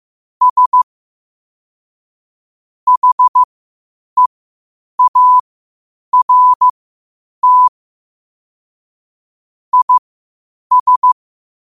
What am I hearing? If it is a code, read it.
Morse code, heard as a keyed tone, about 15 wpm, spaced as for 8 wpm, 1000 Hz. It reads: S HEART IS
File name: qotdfiles / 8wpm